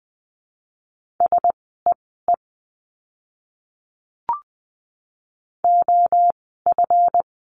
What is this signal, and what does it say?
Morse code, heard as a keyed tone, about 20 wpm, spaced as for 14 wpm, 700 Hz. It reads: SEE OF
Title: 237 Lesson 38f - Letter q - Words 20wpm